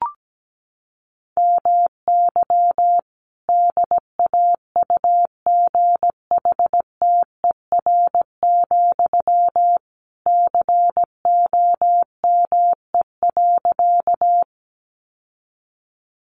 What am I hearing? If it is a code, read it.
Morse code, heard as a keyed tone, about 17 wpm, 700 Hz. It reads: MY DAUGHTER, COME.
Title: Sentences from Top 1500 Words 17wpm